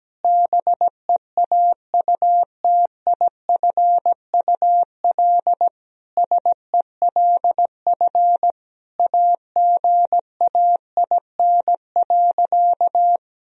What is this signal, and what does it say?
Morse code, heard as a keyed tone, about 17 wpm, 700 Hz. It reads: BEAUTIFUL SELF AGAIN.
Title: Sentences from Top 1000 Words - Review Part 1 - 17wpm